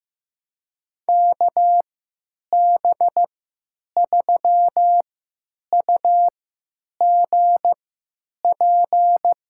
Morse code, heard as a keyed tone, about 15 wpm, spaced as for 9 wpm, 700 Hz. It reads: KB3UGP
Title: USA Call Signs - 6 Characters - 3x Character Spacing - 15wpm